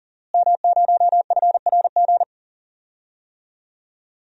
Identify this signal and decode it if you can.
Morse code, heard as a keyed tone, about 40 wpm, 700 Hz. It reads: M0FRG
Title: DXCC-100 Call Signs - 5 Characters - Repeated 2x - 40wpm